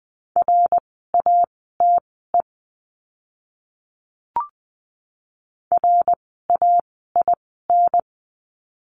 Morse code, heard as a keyed tone, about 20 wpm, spaced as for 14 wpm, 700 Hz. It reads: RATE RAIN
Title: 054 Lesson 10f - Letter r - Words 20wpm